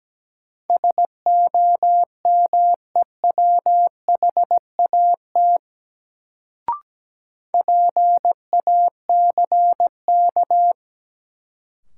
Morse code, heard as a keyed tone, about 17 wpm, 700 Hz. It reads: SOMEWHAT PACK
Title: Sentences from Top 1300 Words - New Word Drill Alt - 17wpm